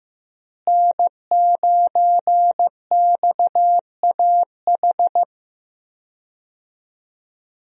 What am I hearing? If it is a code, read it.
Morse code, heard as a keyed tone, about 15 wpm, 700 Hz. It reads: N9XAH
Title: USA Call Signs - 5 Characters - Repeated 2x - 15wpm